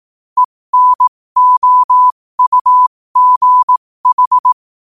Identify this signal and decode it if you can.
Morse code, heard as a keyed tone, about 18 wpm, spaced as for 15 wpm, 1000 Hz. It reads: ENOUGH
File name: qotdfiles / 15wpm